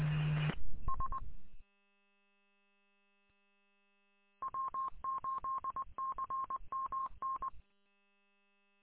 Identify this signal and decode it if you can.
Morse code, heard as a keyed tone, about 20 wpm, 1070 Hz. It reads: S W8CMN
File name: w8cmn- -combined